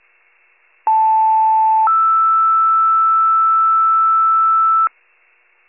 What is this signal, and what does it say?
Two-tone page, heard as a one-second tone, then a three-second tone, about 880 then 1320 Hz, with noise. It is heard from 0.9 to 4.9 s.
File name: wcalert